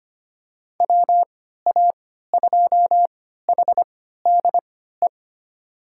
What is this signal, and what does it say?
Morse code, heard as a keyed tone, about 25 wpm, spaced as for 14 wpm, 700 Hz. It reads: WA2HDE
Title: USA Call Signs - 6 Characters - 3x Character Spacing - 25wpm